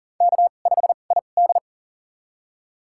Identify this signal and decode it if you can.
Morse code, heard as a keyed tone, about 40 wpm, spaced as for 29 wpm, 700 Hz. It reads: K5ID
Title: USA Call Signs - 4 Characters - 2x Character Spacing - 40wpm